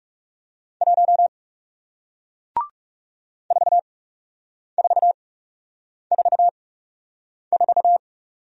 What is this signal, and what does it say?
Morse code, heard as a keyed tone, about 41 wpm, spaced as for 8 wpm, 700 Hz. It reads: 1 4444